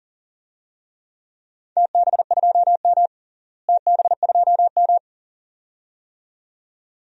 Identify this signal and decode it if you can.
Morse code, heard as a keyed tone, about 40 wpm, 700 Hz. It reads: TB2M TB2M
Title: DXCC-100 Call Signs - 4 Characters - Repeated 2x - 40wpm